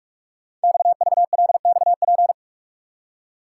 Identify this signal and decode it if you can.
Morse code, heard as a keyed tone, about 44 wpm, 700 Hz. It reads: XVLXP
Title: Code Groups - 5 Letters - 45wpm